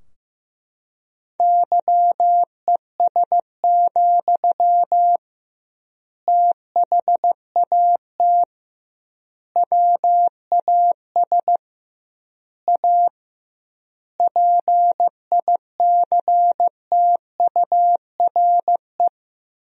Morse code, heard as a keyed tone, about 15 wpm, 700 Hz. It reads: YES, THAT WAS A PICTURE
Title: 2x Word-Spacing - Sentences from Top 500 Words - 15wpm